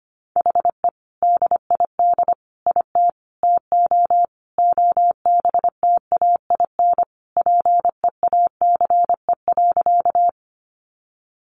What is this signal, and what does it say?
Morse code, heard as a keyed tone, about 25 wpm, 700 Hz. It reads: HE DID IT TO OBTAIN PEACE.
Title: Sentences from Top 1500 Words 25wpm